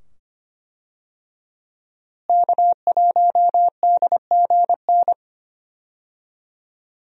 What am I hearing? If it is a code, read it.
Morse code, heard as a keyed tone, about 25 wpm, 700 Hz. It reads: K1DGN
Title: USA Call Signs - 5 Characters - Repeated 2x - 25wpm